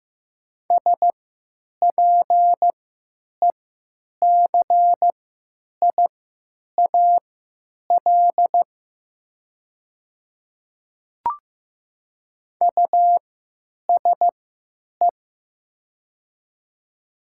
Morse code, heard as a keyed tone, about 15 wpm, spaced as for 9 wpm, 700 Hz. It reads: SPECIAL USE
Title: 3x Character Spacing - Top 500 Words - 15wpm